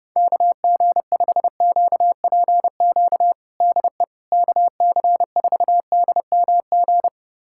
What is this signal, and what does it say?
Morse code, heard as a keyed tone, about 30 wpm, 700 Hz. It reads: KG5QPQ DE KC4DMG
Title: USA Callsign DE Callsign - 6 Characters - 30wpm